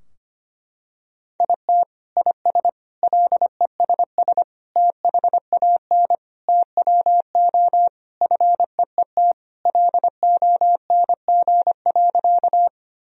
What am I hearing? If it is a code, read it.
Morse code, heard as a keyed tone, about 25 wpm, 700 Hz. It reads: IT IS LESS THAN TWO FEET LONG.
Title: Sentences from Top 1000 Words - Review Part 2 - 25wpm